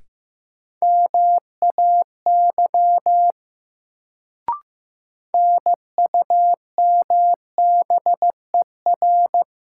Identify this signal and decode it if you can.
Morse code, heard as a keyed tone, about 15 wpm, 700 Hz. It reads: MAY NUMBER
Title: Sentences from Top 100 Words - New Word Drill - 15wpm